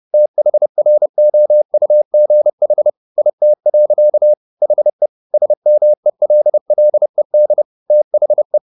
Morse code, heard as a keyed tone, about 30 wpm, 600 Hz. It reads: THROUGH IT. HE SMELLED THE
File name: the old man and the sea / the old man and the sea 1 30wpm